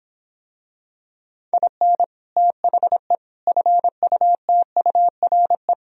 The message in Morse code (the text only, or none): IN THE FUTURE